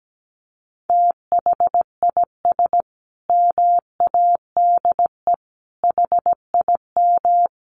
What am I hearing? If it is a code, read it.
Morse code, heard as a keyed tone, about 17 wpm, 700 Hz. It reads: THIS MADE HIM
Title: Sentences from Top 500 Words 17wpm